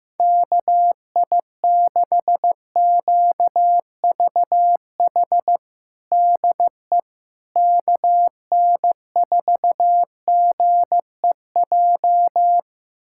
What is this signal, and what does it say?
Morse code, heard as a keyed tone, about 15 wpm, 700 Hz. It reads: KI6QVH DE KN4GEJ
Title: USA Callsign DE Callsign 15wpm